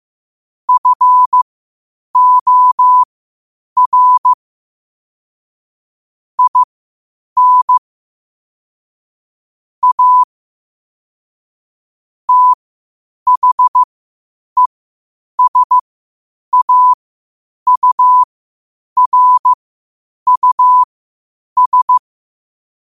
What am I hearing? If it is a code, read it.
Morse code, heard as a keyed tone, about 15 wpm, spaced as for 8 wpm, 1000 Hz. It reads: FOR IN A THESAURUS
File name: qotdfiles / 8wpm